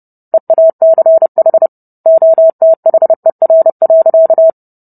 Morse code, heard as a keyed tone, about 30 wpm, 650 Hz. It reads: EACH OTHER.